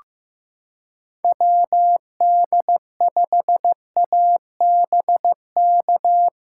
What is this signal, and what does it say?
Morse code, heard as a keyed tone, about 15 wpm, 700 Hz. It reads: WD5ABK
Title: USA Callsign DE Callsign 15wpm